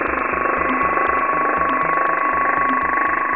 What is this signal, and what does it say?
PSK31, about 31.25 baud.